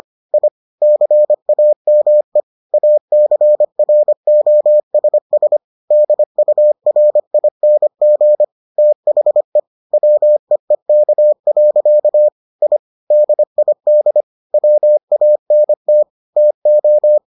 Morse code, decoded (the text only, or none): I CAME ACROSS DURING THE WEEK. I DID WANT TO